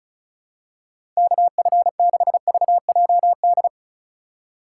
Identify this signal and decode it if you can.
Morse code, heard as a keyed tone, about 34 wpm, 700 Hz. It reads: KF6VJD
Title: USA Call Signs - 6 Characters - 4x Character Spacing - 35wpm